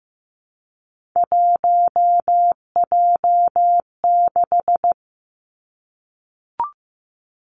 Morse code, heard as a keyed tone, about 15 wpm, 700 Hz. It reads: 1J6
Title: Three Challenging Letter-Number Combinations 15wpm